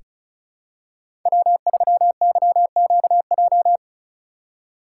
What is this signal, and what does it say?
Morse code, heard as a keyed tone, about 35 wpm, 700 Hz. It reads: W3YQJ
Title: Code Groups - 5 Letter:Numbers - 35wpm